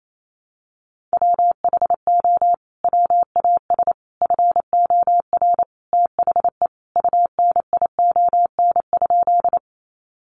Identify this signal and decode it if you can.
Morse code, heard as a keyed tone, about 28 wpm, 700 Hz. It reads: WHO WAS FOR THE UNION?
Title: Sentences from Top 1400 Words 28wpm